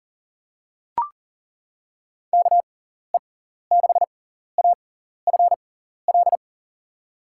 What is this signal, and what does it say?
Morse code, heard as a keyed tone, about 40 wpm, spaced as for 14 wpm, 700 Hz. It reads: KE6AFL
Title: USA Call Signs - 6x Character Spacing - 40wpm